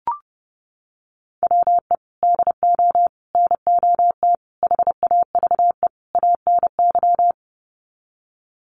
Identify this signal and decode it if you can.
Morse code, heard as a keyed tone, about 30 wpm, 700 Hz. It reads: WE DO NOT HAVE ANY
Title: Sets of 5 Words 30wpm